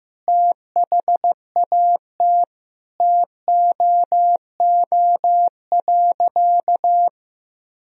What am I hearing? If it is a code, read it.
Morse code, heard as a keyed tone, about 16 wpm, 700 Hz. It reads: THAT TOO.
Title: Sentences from Top 1000 Words - Review Part 1 - 15wpm